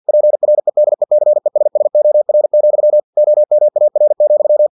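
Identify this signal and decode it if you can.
Morse code, heard as a keyed tone, about 50 wpm, 600 Hz. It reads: PREDECESSOR, OMAR,
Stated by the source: Morse practice text from an English-language novel at 49 wpm.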